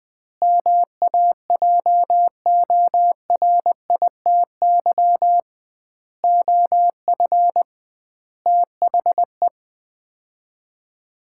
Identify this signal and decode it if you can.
Morse code, heard as a keyed tone, about 20 wpm, 700 Hz. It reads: MAJORITY OF THE
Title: 2x Word-Spacing - Sets of 4 Words - 20wpm